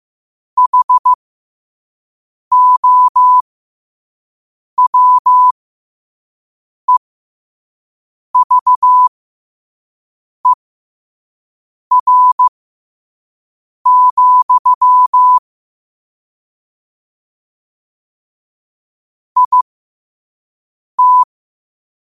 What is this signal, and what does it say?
Morse code, heard as a keyed tone, about 15 wpm, spaced as for 5 wpm, 1000 Hz. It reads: HOWEVER, IT
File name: qotdfiles / 5wpm